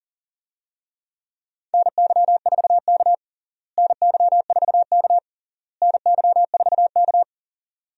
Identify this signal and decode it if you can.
Morse code, heard as a keyed tone, about 37 wpm, 700 Hz. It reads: NY4K NY4K NY4K
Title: USA Call Signs - 4 Characters - Repeated 3x - 40wpm